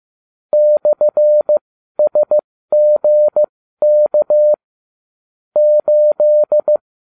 Morse code, heard as a keyed tone, about 15 wpm, spaced as for 12 wpm, 600 Hz. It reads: /SGK 8